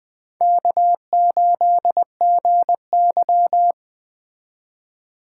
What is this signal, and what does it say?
Morse code, heard as a keyed tone, about 20 wpm, 700 Hz. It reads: K8GY
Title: Speed Racing - USA 1x2 Call Signs 17wpm